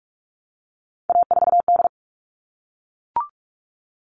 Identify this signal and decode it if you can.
Morse code, heard as a keyed tone, about 44 wpm, 700 Hz. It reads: A4D
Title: All Cut Numbers - 3-Digit 45wpm